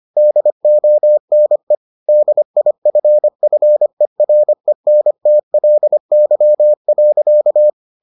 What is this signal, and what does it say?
Morse code, heard as a keyed tone, about 25 wpm, 600 Hz. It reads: DONE DIFFERENTLY.